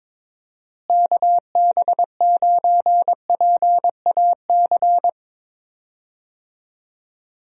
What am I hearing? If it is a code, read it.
Morse code, heard as a keyed tone, about 22 wpm, 700 Hz. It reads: KB9PAC
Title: USA Call Signs - 6 Characters - Repeated 2x - 22wpm